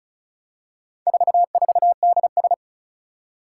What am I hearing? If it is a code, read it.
Morse code, heard as a keyed tone, about 36 wpm, 700 Hz. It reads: 44DS
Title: Four Challenging Letter-Number Combinations 35wpm